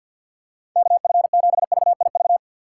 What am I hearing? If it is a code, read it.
Morse code, heard as a keyed tone, about 53 wpm, 700 Hz. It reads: KF7VIV
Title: USA Call Signs - 3x Character Spacing - 50wpm